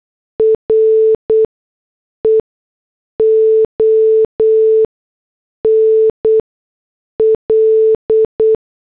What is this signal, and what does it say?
Morse code, heard as a keyed tone, about 8 wpm, spaced as for 6 wpm, 440 Hz. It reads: RE ONL